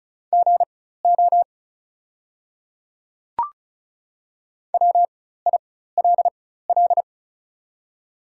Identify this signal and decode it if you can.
Morse code, heard as a keyed tone, about 34 wpm, spaced as for 16 wpm, 690 Hz. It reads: GO WILL
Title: 4x Character Spacing - Top 100 Words - 35wpm